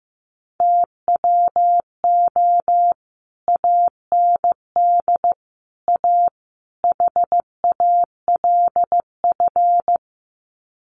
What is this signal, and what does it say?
Morse code, heard as a keyed tone, about 15 wpm, 700 Hz. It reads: TWO AND A HALF
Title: Sets of 4 Words 15wpm